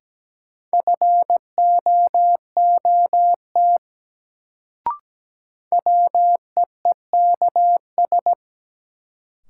Morse code, heard as a keyed tone, about 17 wpm, 700 Hz. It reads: FOOT WEEKS
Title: Sets of 5 Words - New Word Drill - 17wpm